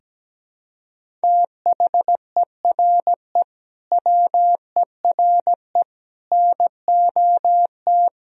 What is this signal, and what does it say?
Morse code, heard as a keyed tone, about 17 wpm, 700 Hz. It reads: THERE WERE NOT
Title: Sentences from Top 1000 Words - Review Part 1 - 17wpm